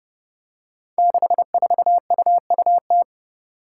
Morse code, heard as a keyed tone, about 31 wpm, 700 Hz. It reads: 64UUT
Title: Five Challenging Letter-Number Combinations 30wpm